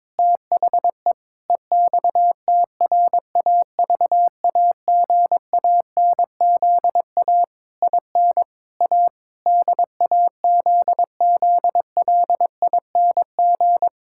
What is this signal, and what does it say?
Morse code, heard as a keyed tone, about 22 wpm, 700 Hz. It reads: THE EXTRAVAGANZA IN A DAZZLING
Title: Pangram Challenge v2 - 22wpm